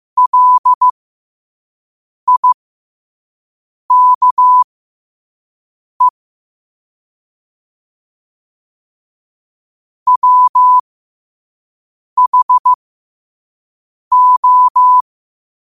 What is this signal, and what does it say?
Morse code, heard as a keyed tone, about 15 wpm, spaced as for 5 wpm, 1000 Hz. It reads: LIKE WHO
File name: qotdfiles / 5wpm